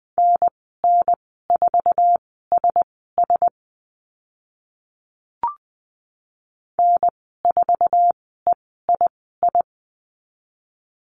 Morse code, heard as a keyed tone, about 20 wpm, spaced as for 14 wpm, 700 Hz. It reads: NN4SS N4EII
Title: 055 Lesson 10g - Letter r - Call Signs 20wpm